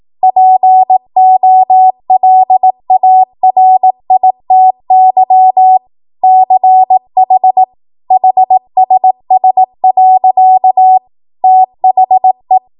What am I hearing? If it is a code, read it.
Morse code, heard as a keyed tone, about 18 wpm, 750 Hz. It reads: POLARITY CH HSS. THE
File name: Morse / 18WPM